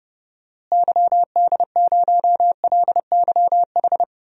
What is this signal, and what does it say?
Morse code, heard as a keyed tone, about 30 wpm, 700 Hz. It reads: YD0LYH
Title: DXCC-100 Call Signs - 6 Characters - Repeated 2x - 30wpm